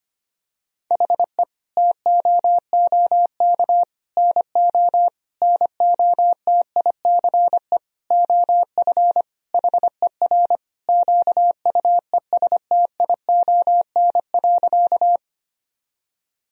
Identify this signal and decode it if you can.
Morse code, heard as a keyed tone, about 25 wpm, 700 Hz. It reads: HE TOOK NO NOTICE OF HER QUESTION.
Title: Sentences from Top 1000 Words - Review Part 2 - 25wpm